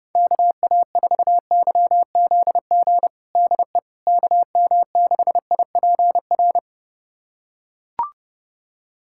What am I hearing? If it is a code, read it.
Morse code, heard as a keyed tone, about 30 wpm, 700 Hz. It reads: KA4YZG DE KM6IPR
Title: USA Callsign DE Callsign - 6 Characters - 30wpm